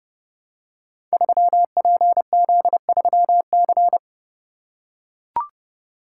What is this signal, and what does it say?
Morse code, heard as a keyed tone, about 30 wpm, 700 Hz. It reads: 3PZ3C
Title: Code Groups - 5 Letter:Numbers - 30wpm